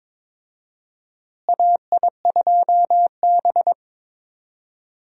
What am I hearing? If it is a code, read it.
Morse code, heard as a keyed tone, about 22 wpm, 700 Hz. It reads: AI2B
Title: USA Call Signs - 4 Characters - Repeated 2x - 22wpm